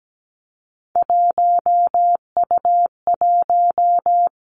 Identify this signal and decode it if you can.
Morse code, heard as a keyed tone, about 17 wpm, 700 Hz. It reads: 1U1
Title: Three Challenging Letter-Number Combinations 17wpm